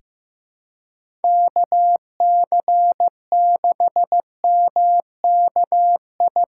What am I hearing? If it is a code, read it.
Morse code, heard as a keyed tone, about 15 wpm, 700 Hz. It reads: KC6MKI
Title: USA Call Signs - 6 Characters - 2x Character Spacing - 15wpm